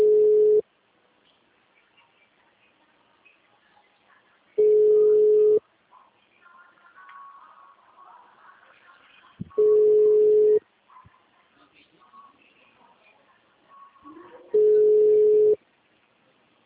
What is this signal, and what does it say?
Ringback tone (Europe, 425 Hz).